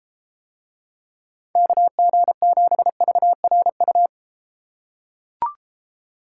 Morse code, heard as a keyed tone, about 31 wpm, 700 Hz. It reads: KG7VRU